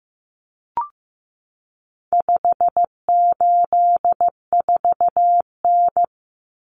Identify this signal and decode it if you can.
Morse code, heard as a keyed tone, about 15 wpm, 700 Hz. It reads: 584N
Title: Common Cut Numbers - 4-Digit 15wpm